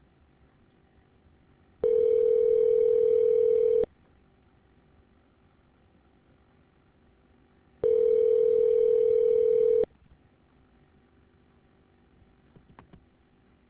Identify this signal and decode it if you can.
Ringback tone (North America), 440 + 480 Hz.